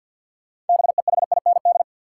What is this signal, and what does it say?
Morse code, heard as a keyed tone, about 52 wpm, 700 Hz. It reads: BEHIND
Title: Sentences from Top 500 Words - New Word Drill - 50wpm